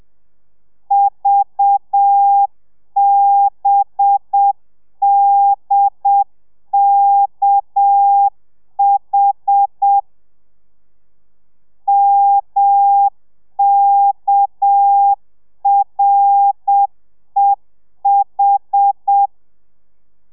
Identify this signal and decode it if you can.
Morse code, heard as a keyed tone, about 7 wpm, 800 Hz. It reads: VBDKH MKREH